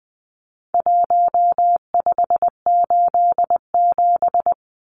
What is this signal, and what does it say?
Morse code, heard as a keyed tone, about 20 wpm, 700 Hz. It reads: 1587
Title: Numbers - 4-Digit 20wpm